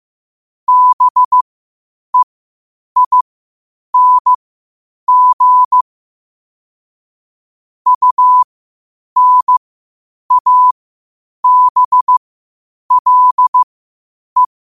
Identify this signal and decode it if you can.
Morse code, heard as a keyed tone, about 15 wpm, spaced as for 8 wpm, 1000 Hz. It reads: BEING UNABLE
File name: qotdfiles / 8wpm